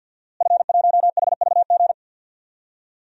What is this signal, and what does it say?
Morse code, heard as a keyed tone, about 46 wpm, 700 Hz. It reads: F1HVG
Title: DXCC-100 Call Signs - 5 Characters - Repeated 2x - 50wpm